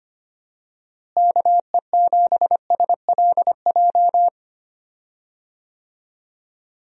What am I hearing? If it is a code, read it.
Morse code, heard as a keyed tone, about 25 wpm, 700 Hz. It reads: KE7SLJ
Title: USA Call Signs - 6 Characters - Repeated 3x - 25wpm